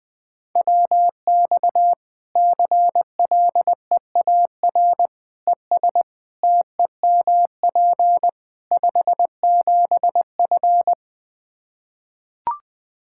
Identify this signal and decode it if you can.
Morse code, heard as a keyed tone, about 20 wpm, 700 Hz. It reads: WX CLEAR ES TEMP 57F
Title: Practice QSO - Exchange 2 - Info - 20wpm